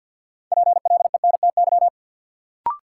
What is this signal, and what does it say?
Morse code, heard as a keyed tone, about 49 wpm, 700 Hz. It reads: PLENTY
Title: Sentences from Top 1300 Words - New Word Drill - 50wpm